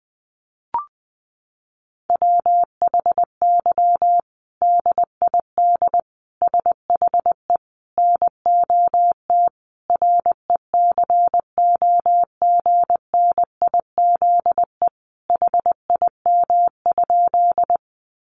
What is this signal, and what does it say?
Morse code, heard as a keyed tone, about 20 wpm, 700 Hz. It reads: WHY DID SHE NOT RECOGNIZE HIM?
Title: Sentences from Top 1400 Words 20wpm